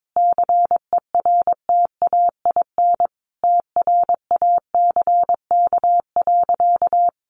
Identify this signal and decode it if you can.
Morse code, heard as a keyed tone, about 22 wpm, 700 Hz. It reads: CERTAIN TRACK.
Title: Sentences from Top 1100 Words 22wpm